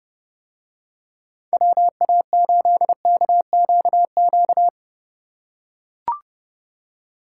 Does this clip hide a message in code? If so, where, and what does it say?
Yes: Morse code from 1.5 to 4.7 s, heard as a keyed tone, about 29 wpm, 700 Hz. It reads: WA8KQQ